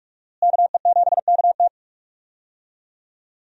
Morse code, heard as a keyed tone, about 46 wpm, 700 Hz. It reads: KE7KT
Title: USA Call Signs - 5 Characters - Repeated 3x - 45wpm